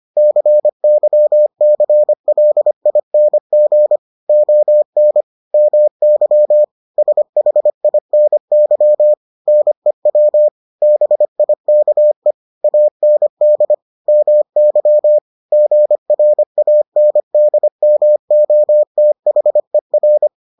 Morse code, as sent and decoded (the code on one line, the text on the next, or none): -.-. -.-- -.-. .-.. .. -. --. / --- -. / -- -.-- / ... .... .. -. -.-- / -. .-- / -... .. -.- . / .- -. -.. / -- -.-- / --. .-. .- -. -.. -- --- - .... .-.
CYCLING ON MY SHINY NEW BIKE AND MY GRANDMOTHER